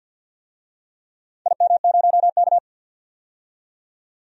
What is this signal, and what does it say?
Morse code, heard as a keyed tone, about 49 wpm, 700 Hz. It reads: IM0K
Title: DXCC-100 Call Signs - 4 Characters - 50wpm